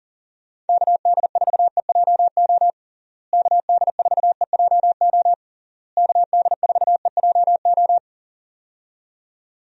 Morse code, heard as a keyed tone, about 40 wpm, 700 Hz. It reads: KD4EJO KD4EJO KD4EJO
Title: USA Call Signs - 6 Characters - Repeated 3x - 40wpm